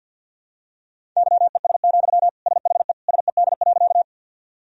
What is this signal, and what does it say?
Morse code, heard as a keyed tone, about 50 wpm, 700 Hz. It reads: YES, SHE SED.
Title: Sentences from Top 500 Words and Abbreviations 50wpm